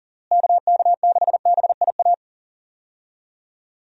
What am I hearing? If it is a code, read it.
Morse code, heard as a keyed tone, about 40 wpm, 700 Hz. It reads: KK6BIA
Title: USA Call Signs - 6 Characters - Repeated 3x - 40wpm